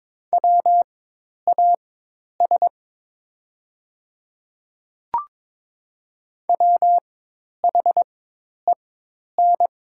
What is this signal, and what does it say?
Morse code, heard as a keyed tone, about 22 wpm, spaced as for 10 wpm, 700 Hz. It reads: WAS WHEN
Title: 4x Character Spacing - Top 100 Words - 22wpm